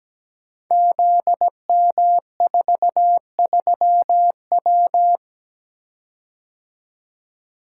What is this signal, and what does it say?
Morse code, heard as a keyed tone, about 17 wpm, 700 Hz. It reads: ZM43W
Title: Code Groups - 5 Letter:Numbers - 17wpm